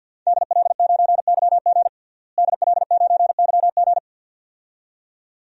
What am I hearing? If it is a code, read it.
Morse code, heard as a keyed tone, about 48 wpm, 700 Hz. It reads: DL9YG DL9YG
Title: DXCC-100 Call Signs - 5 Characters - Repeated 2x - 50wpm